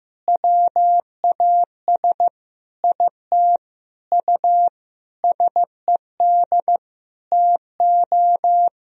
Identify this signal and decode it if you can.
Morse code, heard as a keyed tone, about 15 wpm, 700 Hz. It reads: WAS IT U SED TO
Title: Sentences from Top 200 Words and Abbreviations 15wpm